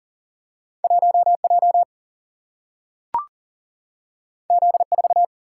Morse code, heard as a keyed tone, about 38 wpm, 700 Hz. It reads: 1J Z4
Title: Single-Letter Single-Number 40wpm